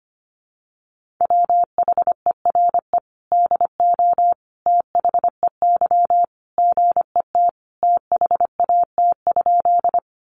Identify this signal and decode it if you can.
Morse code, heard as a keyed tone, about 25 wpm, 700 Hz. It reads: WHERE DO THEY GET THAT?
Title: Sentences from Top 200 Words 25wpm